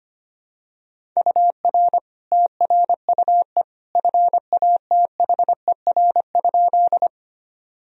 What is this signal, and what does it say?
Morse code, heard as a keyed tone, about 25 wpm, 700 Hz. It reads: UR TRUE FATHER?
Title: Sentences from Top 400 Words and Abbreviations 25wpm